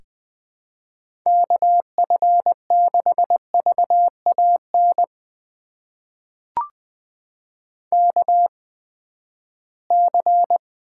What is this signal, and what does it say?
Morse code, heard as a keyed tone, about 21 wpm, spaced as for 14 wpm, 700 Hz. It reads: KF6VAN KC
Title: USA Call Signs - 6 Characters - 8x Character Spacing - 20wpm